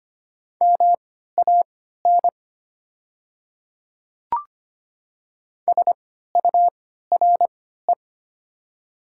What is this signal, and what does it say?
Morse code, heard as a keyed tone, about 25 wpm, spaced as for 14 wpm, 690 Hz. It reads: MAN SURE
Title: 3x Character Spacing - Top 300 Words - 25wpm